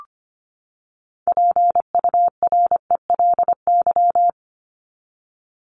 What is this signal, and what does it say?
Morse code, heard as a keyed tone, about 25 wpm, 700 Hz. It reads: PURELY
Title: LY Suffix - Top 100 Words with Base Words 25wpm